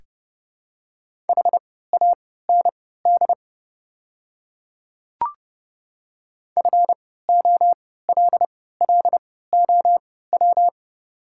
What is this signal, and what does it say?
Morse code, heard as a keyed tone, about 30 wpm, spaced as for 17 wpm, 700 Hz. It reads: HAND FOLLOW
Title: 3x Character Spacing - Top 200 Words - 30wpm